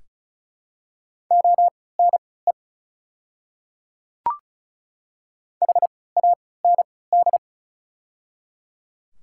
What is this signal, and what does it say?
Morse code, heard as a keyed tone, about 36 wpm, spaced as for 20 wpm, 700 Hz. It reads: ONE HAND
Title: 3x Character Spacing - Top 500 Words - 35wpm